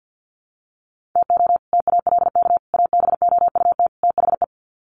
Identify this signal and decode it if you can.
Morse code, heard as a keyed tone, about 50 wpm, 700 Hz. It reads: TO TALK ABOUT THE
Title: Sets of 4 Words 50wpm